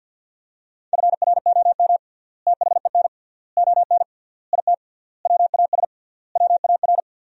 Morse code, heard as a keyed tone, about 50 wpm, 700 Hz. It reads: FROM THEN ON IT WAS WAR
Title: 3x Word-Spacing - Sentences from Top 400 Words - 50wpm